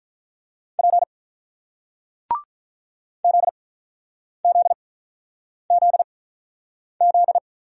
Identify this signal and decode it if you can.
Morse code, heard as a keyed tone, about 48 wpm, spaced as for 9 wpm, 690 Hz. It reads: P ZZZZ